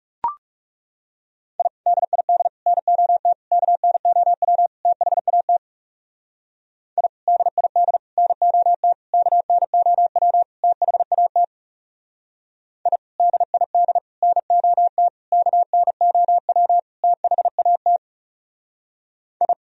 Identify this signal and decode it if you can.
Morse code, heard as a keyed tone, about 35 wpm, 700 Hz. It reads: I DID NOT KNOW THAT I DID NOT KNOW THAT I DID NOT KNOW THAT I
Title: Speed-Racing - Sets of 5 Words 30wpm